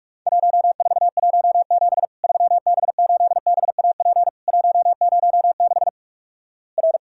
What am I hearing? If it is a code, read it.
Morse code, heard as a keyed tone, about 46 wpm, 700 Hz. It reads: 1417 3B8BAP 106 R
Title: Worked All Europe DX Contest - QTC Exchange - 45wpm